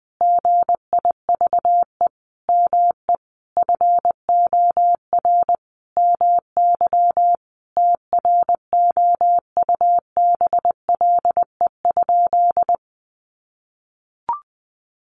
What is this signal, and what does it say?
Morse code, heard as a keyed tone, about 20 wpm, 700 Hz. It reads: GIVE ME FOR MY TROUBLE?
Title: Sentences from Top 1200 Words 20wpm